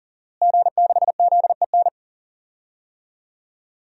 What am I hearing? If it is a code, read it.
Morse code, heard as a keyed tone, about 38 wpm, 700 Hz. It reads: G6ZEN
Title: DXCC-100 Call Signs - 5 Characters - Repeated 3x - 40wpm